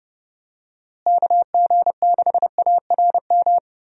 Morse code, heard as a keyed tone, about 32 wpm, 700 Hz. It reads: KG6ARM
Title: USA Call Signs - 6 Characters - Repeated 2x - 30wpm